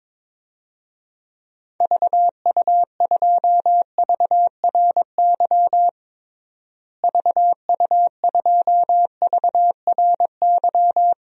Morse code, heard as a keyed tone, about 21 wpm, 700 Hz. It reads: VU2VRY VU2VRY
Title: DXCC-100 Call Signs - 6 Characters - Repeated 2x - 22wpm